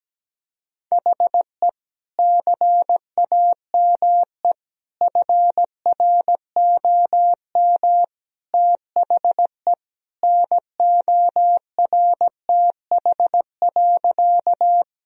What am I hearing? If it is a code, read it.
Morse code, heard as a keyed tone, about 17 wpm, 700 Hz. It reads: HE CAME FROM THE NORTH.
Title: Sentences from Top 1000 Words - Review Part 1 - 17wpm